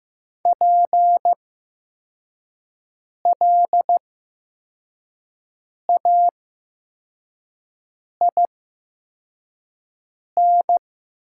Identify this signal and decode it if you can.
Morse code, heard as a keyed tone, about 15 wpm, spaced as for 4 wpm, 700 Hz. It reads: PLAIN
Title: 8x Character Spacing - Top 500 Words - 15wpm